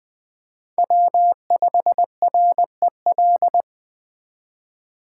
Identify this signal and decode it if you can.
Morse code, heard as a keyed tone, about 24 wpm, 700 Hz. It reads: W5REL
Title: Speed Racing - USA 1x3 Call Signs 20wpm